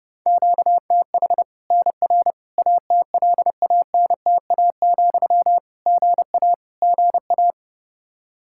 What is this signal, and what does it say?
Morse code, heard as a keyed tone, about 32 wpm, 700 Hz. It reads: QTH NR ATLANTA, GA GA
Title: K1USN SST QSO Practice 30wpm